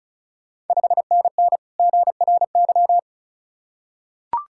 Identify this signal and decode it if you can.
Morse code, heard as a keyed tone, about 36 wpm, 690 Hz. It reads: 5NN GRY
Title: Ontario QSO Party Exchange 35wpm